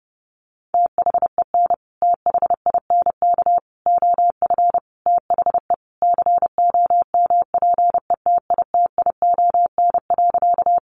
Morse code, heard as a keyed tone, about 30 wpm, 700 Hz. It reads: THEN THINK OF THE COMPETITION.
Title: Sentences from Top 1000 Words 30wpm